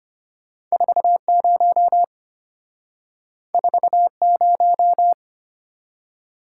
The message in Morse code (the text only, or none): 40 40